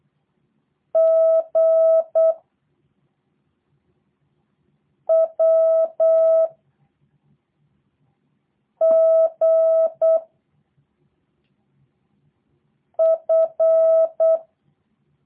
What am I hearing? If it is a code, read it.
Morse code, heard as a keyed tone, about 8 wpm, 650 Hz. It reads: G W G F
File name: jh8cbh / 2023cw / review1-4